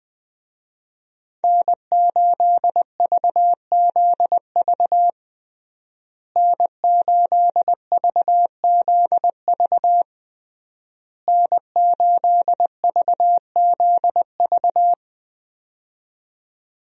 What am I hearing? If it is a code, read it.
Morse code, heard as a keyed tone, about 21 wpm, 700 Hz. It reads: N8VZV N8VZV N8VZV
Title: USA Call Signs - 5 Characters - Repeated 3x - 20wpm